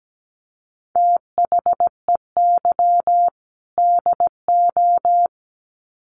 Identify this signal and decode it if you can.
Morse code, heard as a keyed tone, about 17 wpm, 700 Hz. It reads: THEY DO
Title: Sets of 2 Words 17wpm